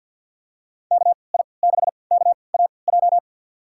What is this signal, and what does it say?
Morse code, heard as a keyed tone, about 52 wpm, spaced as for 29 wpm, 700 Hz. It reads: KI6KAJ
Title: USA Call Signs - 6 Characters - 3x Character Spacing - 50wpm